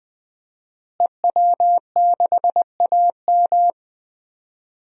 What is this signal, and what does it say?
Morse code, heard as a keyed tone, about 21 wpm, 700 Hz. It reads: EW6AM
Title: DXCC-100 Call Signs - 5 Characters - Repeated 3x - 20wpm